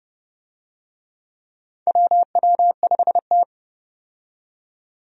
Morse code, heard as a keyed tone, about 30 wpm, 700 Hz. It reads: WW5T
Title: USA Call Signs - 4 Characters - 30wpm